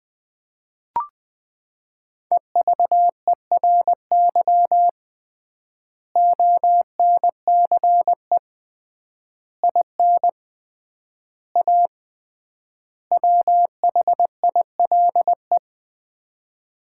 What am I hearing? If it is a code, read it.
Morse code, heard as a keyed tone, about 20 wpm, 700 Hz. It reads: EVERY ONCE IN A WHILE
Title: 3x Word-Spacing - Sets of 5 Words - 20wpm